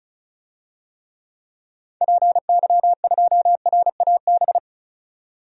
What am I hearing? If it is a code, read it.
Morse code, heard as a keyed tone, about 36 wpm, 700 Hz. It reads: PY2RAB
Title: DXCC-100 Call Signs - 6 Characters - 35wpm